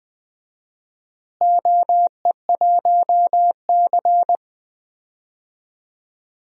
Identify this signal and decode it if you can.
Morse code, heard as a keyed tone, about 20 wpm, 700 Hz. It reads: OE1C
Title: DXCC-100 Call Signs - 4 Characters - Repeated 2x - 20wpm